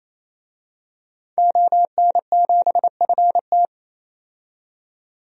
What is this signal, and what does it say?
Morse code, heard as a keyed tone, about 28 wpm, 700 Hz. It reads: ON7FT
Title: DXCC-100 Call Signs - 5 Characters - 28wpm